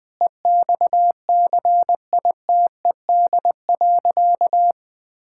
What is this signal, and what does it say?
Morse code, heard as a keyed tone, about 20 wpm, 700 Hz. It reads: EXCITED.
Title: Sentences with 6-Word Phrases - Spoken Beginning - 20wpm